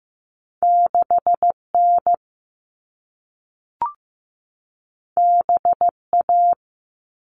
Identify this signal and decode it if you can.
Morse code, heard as a keyed tone, about 15 wpm, 700 Hz. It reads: 6N BA
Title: All Cut Numbers - 2-Digit 15wpm